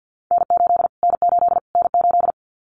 Morse code, heard as a keyed tone, about 50 wpm, 700 Hz. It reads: N8 N8 N8
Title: Callsign Prefixes for Top 25 DXCC - Repeated 3x - 50wpm